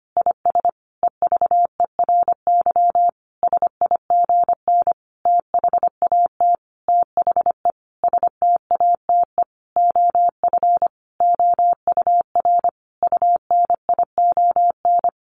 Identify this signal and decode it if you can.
Morse code, heard as a keyed tone, about 25 wpm, 700 Hz. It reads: IS EVERY SIGN THAT THE STATE OF OUR UNION
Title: Sentences from Top 1400 Words 25wpm